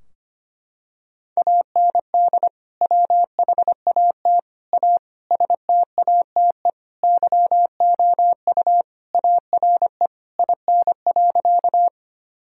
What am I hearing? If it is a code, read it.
Morse code, heard as a keyed tone, about 25 wpm, 700 Hz. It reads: AND WHAT A STATE YOU ARE IN.
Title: Sentences from Top 1000 Words - Review Part 2 - 25wpm